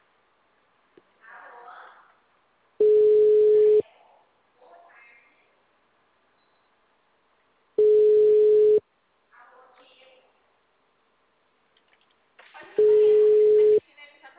Ringback tone (Europe, 425 Hz).